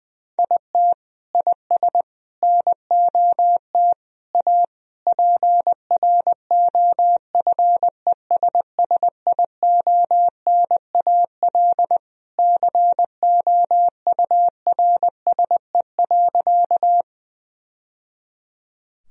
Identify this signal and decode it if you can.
Morse code, heard as a keyed tone, about 20 wpm, 700 Hz. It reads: IT IS NOT A PROFESSIONAL COURSE.
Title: Sentences from Top 1000 Words - Review Part 4 - 20wpm